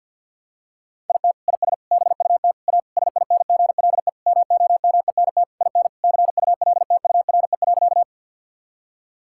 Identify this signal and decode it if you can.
Morse code, heard as a keyed tone, about 50 wpm, 700 Hz. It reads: IT IS BUT A SINGLE MOMENT IN CULTURE.
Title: Sentences from Top 1000 Words - Review Part 6 - 50wpm